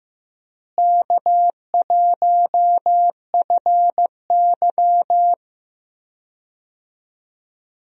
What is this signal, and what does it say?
Morse code, heard as a keyed tone, about 15 wpm, 700 Hz. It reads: K1FY
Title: USA Call Signs - 4 Characters - Repeated 2x - 15wpm